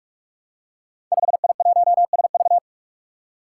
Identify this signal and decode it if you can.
Morse code, heard as a keyed tone, about 44 wpm, 700 Hz. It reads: 5I1SV
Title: Five Challenging Letter-Number Combinations 45wpm